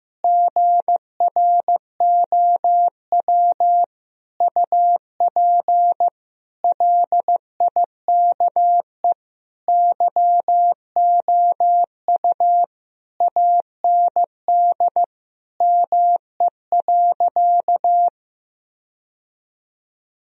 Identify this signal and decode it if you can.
Morse code, heard as a keyed tone, about 16 wpm, 700 Hz. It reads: GROW UP LIKE YOU AND ME.
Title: Sentences from Top 1000 Words - Review Part 1 - 15wpm